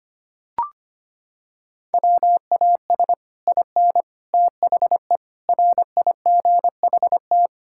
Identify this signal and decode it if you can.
Morse code, heard as a keyed tone, about 25 wpm, 700 Hz. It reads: WAS IN THE RIGHT
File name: Sets of 9 Words - 25wpm